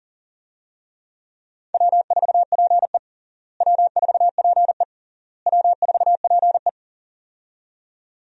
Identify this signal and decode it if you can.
Morse code, heard as a keyed tone, about 40 wpm, 700 Hz. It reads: W4PE W4PE W4PE
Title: USA Call Signs - 4 Characters - Repeated 3x - 40wpm